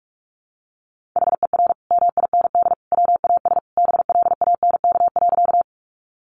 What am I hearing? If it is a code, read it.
Morse code, heard as a keyed tone, about 44 wpm, 700 Hz. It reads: HER MIND WAS BLANK.
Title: Sentences from Top 1500 Words 45wpm